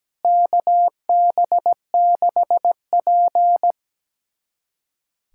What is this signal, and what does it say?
Morse code, heard as a keyed tone, about 17 wpm, 700 Hz. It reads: KB6P
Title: USA Callsign DE Callsign - 4 Characters - 17wpm